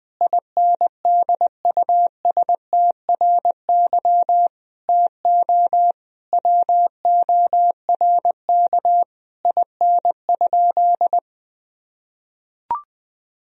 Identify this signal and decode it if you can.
Morse code, heard as a keyed tone, about 20 wpm, 700 Hz. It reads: INDUSTRY TO WORK IN?
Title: Sentences from Top 1000 Words - Review Part 4 - 20wpm